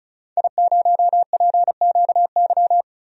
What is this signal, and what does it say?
Morse code, heard as a keyed tone, about 34 wpm, 700 Hz. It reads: I0PQY
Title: DXCC-100 Call Signs - 5 Characters - 35wpm